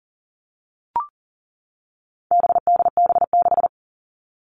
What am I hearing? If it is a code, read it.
Morse code, heard as a keyed tone, about 40 wpm, 700 Hz. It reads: BDB6
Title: All Cut Numbers - 4-Digit 40wpm